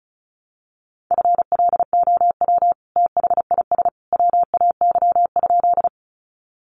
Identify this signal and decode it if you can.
Morse code, heard as a keyed tone, about 34 wpm, 700 Hz. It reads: FLOW THIS WAY?
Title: Sentences from Top 1200 Words 35wpm